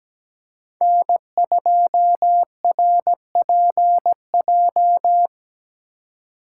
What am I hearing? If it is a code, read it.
Morse code, heard as a keyed tone, about 17 wpm, 700 Hz. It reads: N2RPJ
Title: USA Call Signs - 5 Characters - 2x Character Spacing - 17wpm